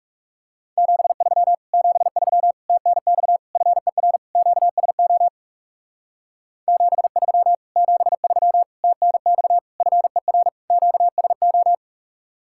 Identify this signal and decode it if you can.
Morse code, heard as a keyed tone, about 36 wpm, 700 Hz. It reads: 73 73 TNX FER QSO 73 73 TNX FER QSO
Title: Speed-Racing Common QSO Phrases - 30wpm